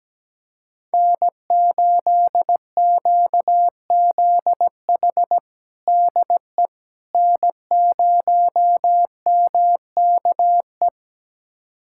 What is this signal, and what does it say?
Morse code, heard as a keyed tone, about 17 wpm, 700 Hz. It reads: N8QZH DE N0MKE
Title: USA Callsign DE Callsign - 5 Characters - 17wpm